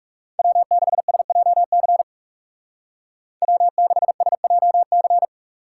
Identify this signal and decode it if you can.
Morse code, heard as a keyed tone, about 40 wpm, 700 Hz. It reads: W6SJC W6SJC